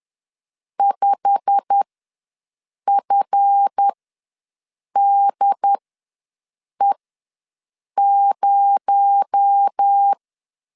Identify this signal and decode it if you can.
Morse code, heard as a keyed tone, about 10 wpm, spaced as for 6 wpm, 800 Hz. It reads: 5FDE0
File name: LEZIONI / 8 YJ45 / 8 YJ45 ws 10wpm 3843